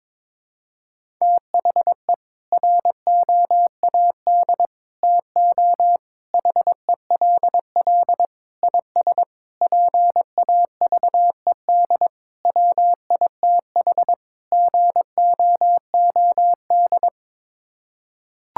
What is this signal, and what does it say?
Morse code, heard as a keyed tone, about 21 wpm, 700 Hz. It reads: THE ROAD TO HELL IS PAVED WITH GOOD
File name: practice / 22wpm / Sets of 8 Words - 22wpm